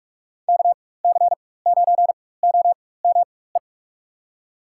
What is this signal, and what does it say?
Morse code, heard as a keyed tone, about 43 wpm, spaced as for 21 wpm, 700 Hz. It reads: KC9OME
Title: USA Call Signs - 4x Character Spacing - 45wpm